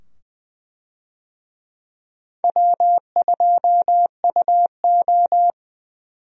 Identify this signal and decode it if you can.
Morse code, heard as a keyed tone, about 20 wpm, 700 Hz. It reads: W2UO
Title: USA Call Signs - 4 Characters - 20wpm